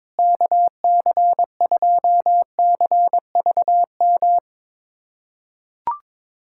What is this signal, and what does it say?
Morse code, heard as a keyed tone, about 22 wpm, 700 Hz. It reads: KC2CVM
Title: USA Callsign DE Callsign - 6 Characters - 22wpm